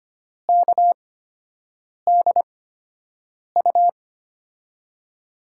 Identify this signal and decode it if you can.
Morse code, heard as a keyed tone, about 25 wpm, spaced as for 7 wpm, 700 Hz. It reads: KDU
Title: USA Call Signs - 5 Characters - 8x Character Spacing - 25wpm